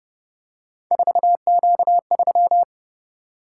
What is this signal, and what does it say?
Morse code, heard as a keyed tone, about 29 wpm, 700 Hz. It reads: 4Q3
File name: Speed-Racing - Three Letter-Number Combinations - 25wpm